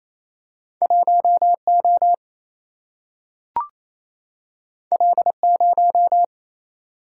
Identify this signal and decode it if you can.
Morse code, heard as a keyed tone, about 28 wpm, 700 Hz. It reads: 1 O L 0
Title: Single-Letter Single-Number 28wpm